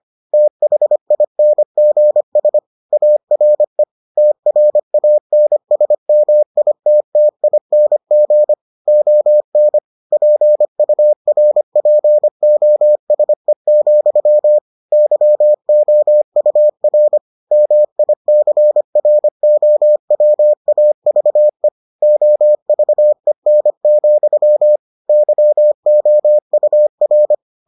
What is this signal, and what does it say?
Morse code, heard as a keyed tone, about 25 wpm, 600 Hz. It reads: THINGS ARE TRANSMITTING ON PURPOSE, YOUR MICROWAVE OVEN, YOUR